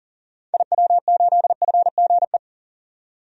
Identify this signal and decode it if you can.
Morse code, heard as a keyed tone, about 40 wpm, 700 Hz. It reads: IW8FGE
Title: DXCC-100 Call Signs - 6 Characters - 40wpm